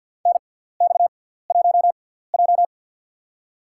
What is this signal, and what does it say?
Morse code, heard as a keyed tone, about 51 wpm, spaced as for 17 wpm, 690 Hz. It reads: NX1J
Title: USA Call Signs - 4 Characters - 6x Character Spacing - 50wpm